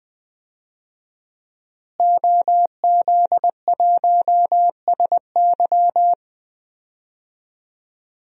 Morse code, heard as a keyed tone, about 20 wpm, 700 Hz. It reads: OZ1SY
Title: DXCC-100 Call Signs - 5 Characters - Repeated 2x - 20wpm